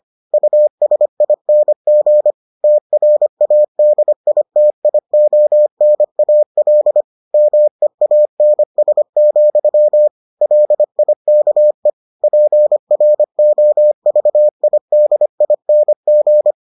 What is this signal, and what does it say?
Morse code, heard as a keyed tone, about 25 wpm, 600 Hz. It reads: USING TRADITIONAL MEANS, LIKE PROVIDING